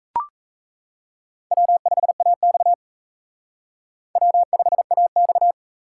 Morse code, heard as a keyed tone, about 34 wpm, 700 Hz. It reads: W5AX W5AX
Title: Speed Racing - USA 1x2 Call Signs 28wpm